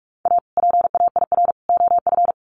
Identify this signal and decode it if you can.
Morse code, heard as a keyed tone, about 44 wpm, 700 Hz. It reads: A PAIR OF
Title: Sets of 3 Words 45wpm